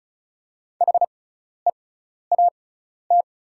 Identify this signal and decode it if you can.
Morse code, heard as a keyed tone, about 36 wpm, spaced as for 12 wpm, 700 Hz. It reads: HEAT
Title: 6x Character Spacing - Top 500 Words - 35wpm